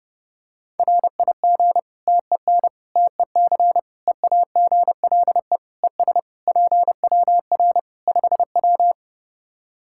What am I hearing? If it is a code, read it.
Morse code, heard as a keyed tone, about 30 wpm, 700 Hz. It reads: RIG TEN TEC EAGLE ES PWR 5W
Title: Practice QSO - Exchange 2 - Info - 30wpm